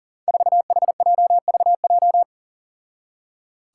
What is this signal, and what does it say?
Morse code, heard as a keyed tone, about 40 wpm, 700 Hz. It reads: 4HJVJ
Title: Five Challenging Letter-Number Combinations 40wpm